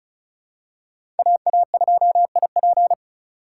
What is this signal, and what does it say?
Morse code, heard as a keyed tone, about 34 wpm, 700 Hz. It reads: AA2IP
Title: USA Call Signs - 5 Characters - 4x Character Spacing - 35wpm